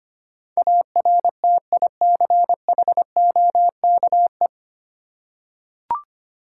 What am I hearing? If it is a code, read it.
Morse code, heard as a keyed tone, about 25 wpm, 700 Hz. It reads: ARTICHOKE
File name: practice / 25wpm / Difficult Words with Hints - 25wpm